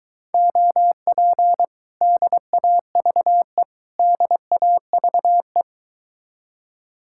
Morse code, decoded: OP DAVE DAVE

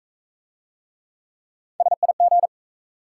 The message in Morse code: SIG

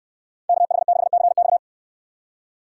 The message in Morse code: KH6CX

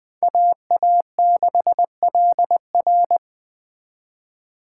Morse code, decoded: AA6LR

